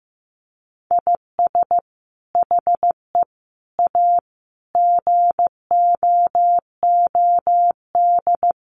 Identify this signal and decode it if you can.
Morse code, heard as a keyed tone, about 15 wpm, 700 Hz. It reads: IS HE A GOOD
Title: Sentences from Top 200 Words 15wpm